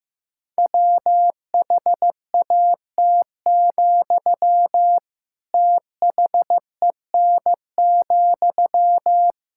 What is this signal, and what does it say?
Morse code, heard as a keyed tone, about 15 wpm, 700 Hz. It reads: WHAT, THEN,
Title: Sentences from Top 1000 Words - Review Part 6 - 15wpm